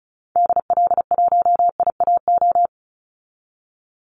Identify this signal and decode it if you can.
Morse code, heard as a keyed tone, about 34 wpm, 700 Hz. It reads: DL1IAO
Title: Call Signs Repeated 2x - 35wpm